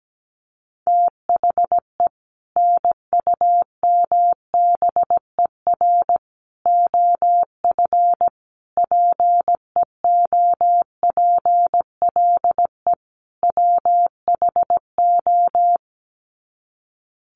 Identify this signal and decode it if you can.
Morse code, heard as a keyed tone, about 17 wpm, 700 Hz. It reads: THE NUMBER OF PEOPLE WHO
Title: Sets of 5 Words Encore 17wpm